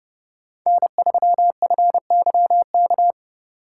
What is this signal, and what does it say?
Morse code, heard as a keyed tone, about 28 wpm, 700 Hz. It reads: N3FYK